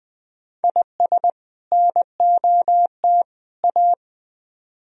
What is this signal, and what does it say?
Morse code, heard as a keyed tone, about 20 wpm, 700 Hz. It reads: IS NOT A
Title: Speed-Racing - Sets of 3 Words 17wpm